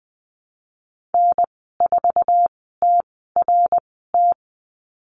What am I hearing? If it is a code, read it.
Morse code, heard as a keyed tone, about 20 wpm, spaced as for 14 wpm, 700 Hz. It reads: N4TRT
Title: 062 Lesson 11g - Letter h - Call Signs 20wpm